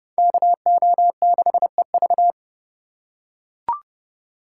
Morse code, heard as a keyed tone, about 29 wpm, 700 Hz. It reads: KO6EV